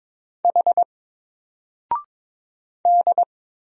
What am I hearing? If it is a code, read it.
Morse code, heard as a keyed tone, about 22 wpm, 700 Hz. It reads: H D
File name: Single-Letter-Number-22wpm